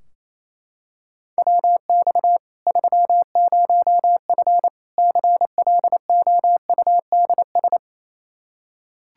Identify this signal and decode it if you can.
Morse code, heard as a keyed tone, about 28 wpm, 700 Hz. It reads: WX 30F CLOUDS
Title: Practice QSO - Exchange 2 - Info - 28wpm